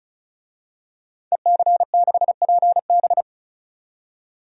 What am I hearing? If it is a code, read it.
Morse code, heard as a keyed tone, about 35 wpm, 700 Hz. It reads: EC6PB
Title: DXCC-100 Call Signs - 5 Characters - 35wpm